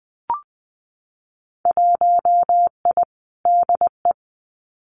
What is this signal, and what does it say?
Morse code, heard as a keyed tone, about 20 wpm, 700 Hz. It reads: 1I DE
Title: Winter Field Day Exchanges Encore 20wpm